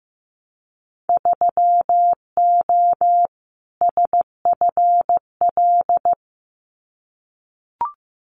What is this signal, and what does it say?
Morse code, heard as a keyed tone, about 15 wpm, 700 Hz. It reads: 3O SFL
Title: Winter Field Day Exchanges 15wpm